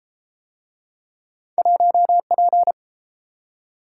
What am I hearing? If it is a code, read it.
Morse code, heard as a keyed tone, about 32 wpm, 700 Hz. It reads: 1P